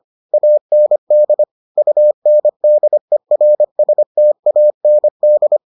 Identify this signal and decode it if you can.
Morse code, heard as a keyed tone, about 25 wpm, 600 Hz. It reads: AND UNDERSTAND